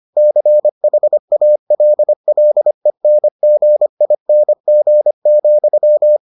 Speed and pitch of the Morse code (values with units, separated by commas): 25 wpm, 600 Hz